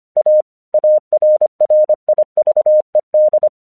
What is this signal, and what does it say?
Morse code, heard as a keyed tone, about 25 wpm, 610 Hz. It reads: A ARRIVED
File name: 1 English 25wpm / AQC17